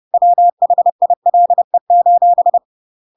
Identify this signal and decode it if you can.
Morse code, heard as a keyed tone, about 30 wpm, 700 Hz. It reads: WHILE:
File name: Pinocchio / Pinocchio ch5 30 30 ik1hge